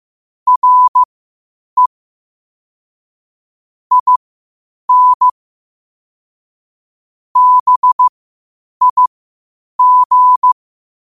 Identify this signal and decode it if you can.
Morse code, heard as a keyed tone, about 15 wpm, spaced as for 8 wpm, 1000 Hz. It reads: RE IN BIG
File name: qotdfiles / 8wpm